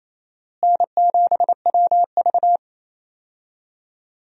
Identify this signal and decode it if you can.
Morse code, heard as a keyed tone, about 27 wpm, 700 Hz. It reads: N7WV
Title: USA Call Signs - 4 Characters - Repeated 2x - 28wpm